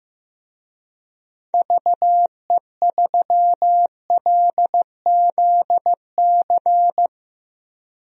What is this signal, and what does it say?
Morse code, heard as a keyed tone, about 16 wpm, 700 Hz. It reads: VE3LZC
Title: DXCC-100 Call Signs - 6 Characters - 15wpm